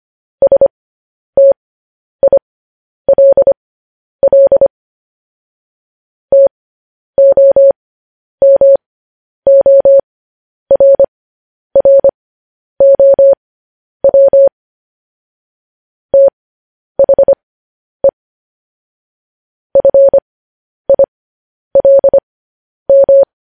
Morse code, heard as a keyed tone, about 25 wpm, spaced as for 10 wpm, 550 Hz. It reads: STILL TOMORROW THE FILM
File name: podcast / News headlines-1x-25x10